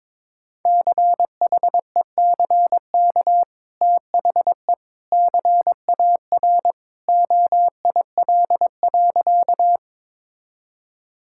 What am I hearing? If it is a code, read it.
Morse code, heard as a keyed tone, about 22 wpm, 700 Hz. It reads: CHECK THE CAR OIL.
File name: practice / 22wpm / Easy Sentences with Hints - 22wpm